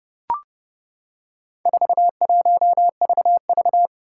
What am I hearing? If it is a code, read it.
Morse code, heard as a keyed tone, about 30 wpm, 700 Hz. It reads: 41VV
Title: Four Challenging Letter-Number Combinations 30wpm